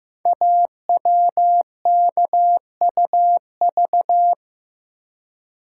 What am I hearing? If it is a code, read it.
Morse code, heard as a keyed tone, about 15 wpm, 700 Hz. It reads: AWKUV
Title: Code Groups - 5 Letters - 15wpm